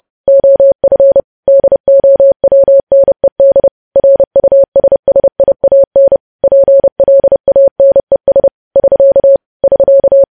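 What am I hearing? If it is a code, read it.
Morse code, heard as a keyed tone, about 30 wpm, 550 Hz. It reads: OF DOWNED RUSSIAN PLANES <SK> <SK>
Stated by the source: News headlines sent in Morse at 30 wpm, preceded by VVV.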